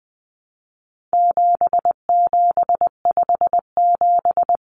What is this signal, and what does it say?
Morse code, heard as a keyed tone, about 20 wpm, 700 Hz. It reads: 7757
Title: Numbers - 4-Digit 20wpm